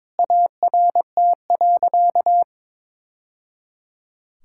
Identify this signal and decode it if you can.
Morse code, heard as a keyed tone, about 22 wpm, 700 Hz. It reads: ART.
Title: Sentences from Top 1000 Words - Review Part 6 - 22wpm